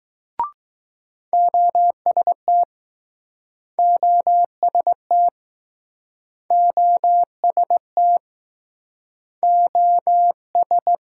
Morse code, heard as a keyed tone, about 19 wpm, 700 Hz. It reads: OST OST OST O S